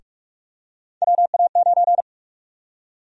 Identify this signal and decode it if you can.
Morse code, heard as a keyed tone, about 48 wpm, 700 Hz. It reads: WA9
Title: USA Call Sign Prefixes - Repeated 2x - 45wpm